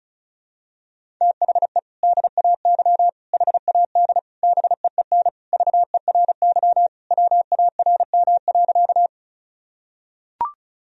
Morse code, heard as a keyed tone, about 36 wpm, 700 Hz. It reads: THE DAY HAD BEEN VERY WARM.
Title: Sentences from Top 1000 Words - Review Part 2 - 35wpm